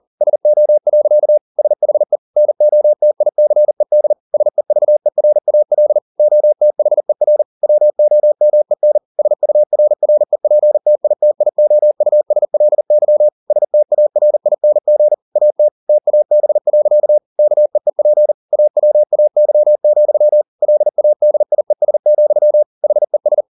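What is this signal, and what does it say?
Morse code, heard as a keyed tone, about 40 wpm, 600 Hz. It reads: SO. SHE NOTICED SEVERAL OTHER WOMEN SURREPTITIOUSLY STARING AT TAB. KEEP AWAY, LADIES, HES
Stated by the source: Morse code at 40 wpm, practice text, CW med Gnister.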